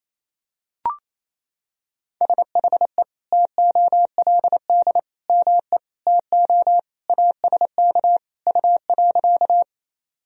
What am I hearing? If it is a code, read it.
Morse code, heard as a keyed tone, about 28 wpm, 700 Hz. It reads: SHE TOLD ME TO ASK U.
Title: Sentences from Top 400 Words and Abbreviations 28wpm